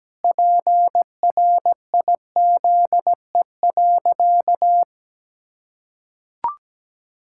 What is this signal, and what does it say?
Morse code, heard as a keyed tone, about 17 wpm, 700 Hz. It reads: PRIZE.
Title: Pangram Challenge v2 - 17wpm